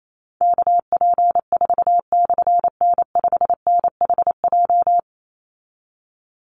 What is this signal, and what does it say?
Morse code, heard as a keyed tone, about 28 wpm, 700 Hz. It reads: KP4/N5NHJ
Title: Call Signs Repeated 3x - 28wpm